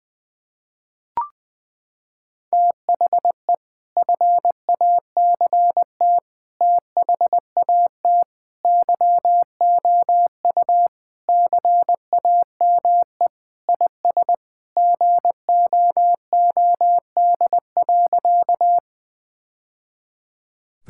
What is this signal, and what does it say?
Morse code, heard as a keyed tone, about 20 wpm, 700 Hz. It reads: THE FACT THAT YOU CAME IS GOOD.
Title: Sentences with 3-Word Phrases - 20wpm